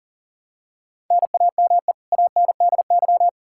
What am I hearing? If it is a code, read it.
Morse code, heard as a keyed tone, about 40 wpm, 700 Hz. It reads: NAME ANDY
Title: Practice QSO - Exchange 1 - Beginning - 40wpm